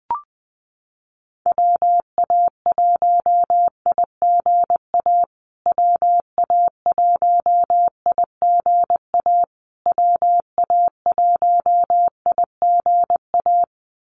Morse code, heard as a keyed tone, about 20 wpm, 700 Hz. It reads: WA1IGA WA1IGA WA1IGA
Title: USA Call Signs Repeated 3x 20wpm